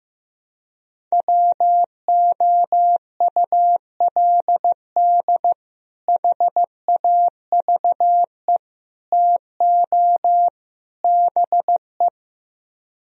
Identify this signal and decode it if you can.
Morse code, heard as a keyed tone, about 15 wpm, 700 Hz. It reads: WOULD HAVE TO BE
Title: Speed-Racing - Sets of 4 Words 15wpm